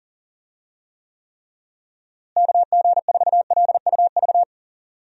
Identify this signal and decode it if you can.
Morse code, heard as a keyed tone, about 40 wpm, 700 Hz. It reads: KG4LUV
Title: USA Call Signs - 6 Characters - Repeated 3x - 40wpm